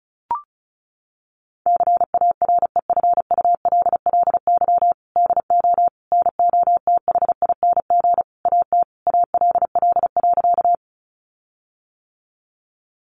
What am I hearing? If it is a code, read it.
Morse code, heard as a keyed tone, about 34 wpm, 700 Hz. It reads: CAREFULLY DO NOTHING AT ALL.
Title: Sentences from Top 1300 Words 35wpm